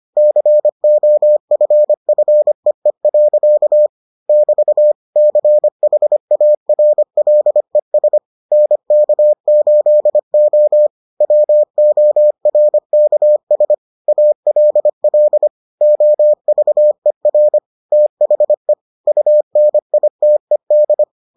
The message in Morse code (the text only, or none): COFFEE. <BT> CHARLES NK8O WORKS ALL OVER THE UNITED